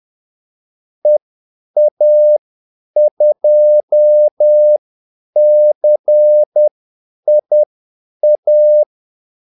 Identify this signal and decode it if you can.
Morse code, heard as a keyed tone, about 10 wpm, spaced as for 8 wpm, 600 Hz. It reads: EA2CIA
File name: Callsigns-10-8wpm